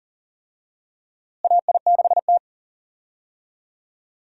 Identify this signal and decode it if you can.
Morse code, heard as a keyed tone, about 40 wpm, 700 Hz. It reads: AI6T
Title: USA Call Signs - 4 Characters - Repeated 3x - 40wpm